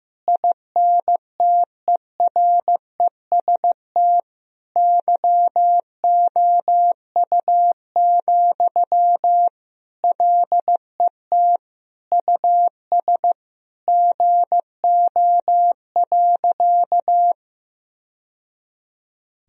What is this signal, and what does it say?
Morse code, heard as a keyed tone, about 16 wpm, 700 Hz. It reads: INTEREST YOU, LET US GO.
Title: Sentences from Top 1000 Words - Review Part 2 - 15wpm